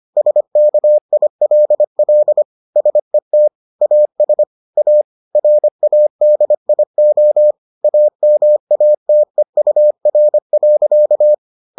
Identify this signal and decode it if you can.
Morse code, heard as a keyed tone, about 25 wpm, 600 Hz. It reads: SKILL SET AS A RADIO AMATEUR.